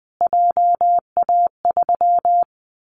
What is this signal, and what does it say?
Morse code, heard as a keyed tone, about 19 wpm, 700 Hz. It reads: JA3
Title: Callsign Prefixes for Top 25 DXCC - Repeated 3x - 20wpm